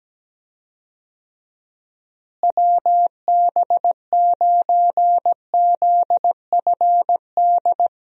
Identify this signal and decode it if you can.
Morse code, heard as a keyed tone, about 17 wpm, 700 Hz. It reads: WB9ZFD
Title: USA Call Signs - 6 Characters - Repeated 2x - 17wpm